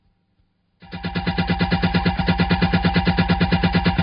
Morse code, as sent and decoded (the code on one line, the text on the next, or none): none